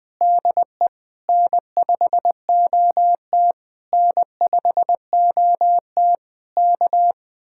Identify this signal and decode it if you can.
Morse code, heard as a keyed tone, about 20 wpm, 700 Hz. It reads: DE N5OT N5OT K